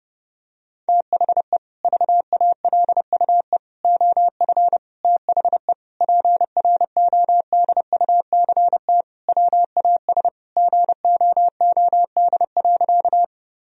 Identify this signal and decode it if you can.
Morse code, heard as a keyed tone, about 30 wpm, 700 Hz. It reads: THE VALUE OF THE PRODUCT WAS GOOD.
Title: Sentences from Top 1000 Words - Review Part 3 - 30wpm